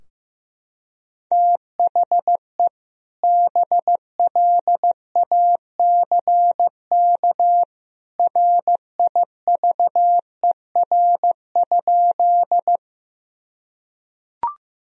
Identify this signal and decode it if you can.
Morse code, heard as a keyed tone, about 15 wpm, 700 Hz. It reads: THE BLACK RIVER?
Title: Sentences from Top 1000 Words - Review Part 2 - 15wpm